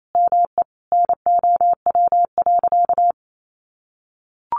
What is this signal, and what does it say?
Morse code, heard as a keyed tone, about 28 wpm, 700 Hz. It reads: ME NOW.
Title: Sentences from Top 200 Words 28wpm